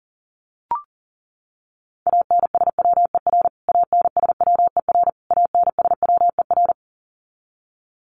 Morse code, heard as a keyed tone, about 40 wpm, 700 Hz. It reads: ANSWER ANSWER ANSWER
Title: Top 500 Words Repeated 40wpm